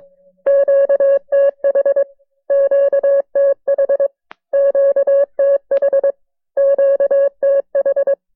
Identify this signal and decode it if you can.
Morse code, heard as a keyed tone, about 20 wpm, 590 Hz. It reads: QTH QTH QTH QTH
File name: FreeFiles / CW-practice / KB6MT-tape1-side2